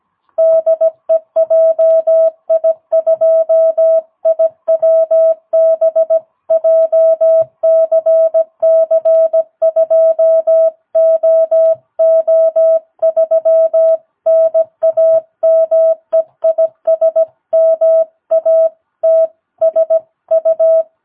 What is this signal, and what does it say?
Morse code, heard as a keyed tone, about 17 wpm, 660 Hz. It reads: DE JI2IWB JCC2OO3 NAME IS MATSU
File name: jh8cbh / 2022cw